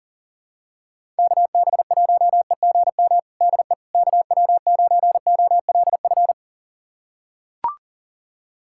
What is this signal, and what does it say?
Morse code, heard as a keyed tone, about 37 wpm, 700 Hz. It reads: KB1EGM DE KW9OLF
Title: USA Callsign DE Callsign 40wpm